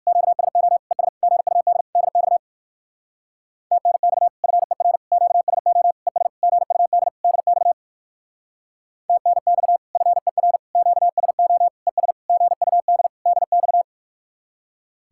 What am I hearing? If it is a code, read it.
Morse code, heard as a keyed tone, about 58 wpm, 700 Hz. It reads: QSO ES GUD DX TNX FER QSO ES GUD DX TNX FER QSO ES GUD DX